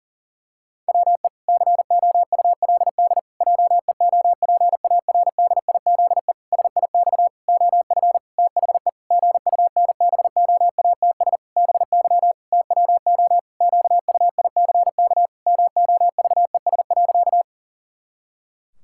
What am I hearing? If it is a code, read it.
Morse code, heard as a keyed tone, about 40 wpm, 700 Hz. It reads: WE COULD JEOPARDIZE SIX OF THE GUNBOATS BY TWO QUICK MOVES.
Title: Pangram Challenge v2 - 40wpm